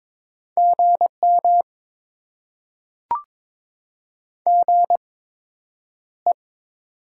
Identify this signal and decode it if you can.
Morse code, heard as a keyed tone, about 22 wpm, spaced as for 9 wpm, 700 Hz. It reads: GM GE
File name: Top 20 QSO Elements - 8x Character Spacing - 22wpm